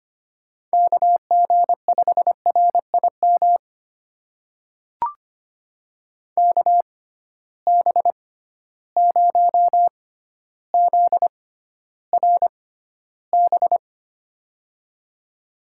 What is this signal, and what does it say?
Morse code, heard as a keyed tone, about 25 wpm, spaced as for 13 wpm, 700 Hz. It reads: KG5RIM KB0ZRB
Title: USA Call Signs - 6 Characters - 6x Character Spacing - 25wpm